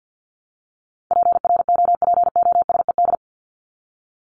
Morse code, heard as a keyed tone, about 50 wpm, 700 Hz. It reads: PROPOSED
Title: ED Suffix - Top 100 Words with Base Words 50wpm